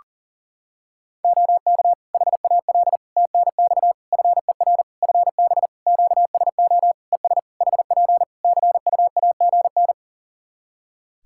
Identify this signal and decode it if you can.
Morse code, heard as a keyed tone, about 40 wpm, 700 Hz. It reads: OK HAL TNX FER FB QSO ES HP CUAGN
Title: Practice QSO - Exchange 4 - Ending - 40wpm